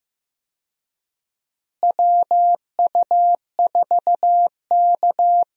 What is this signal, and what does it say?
Morse code, heard as a keyed tone, about 15 wpm, 700 Hz. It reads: WU4K
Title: USA Call Signs - 4 Characters - Repeated 3x - 15wpm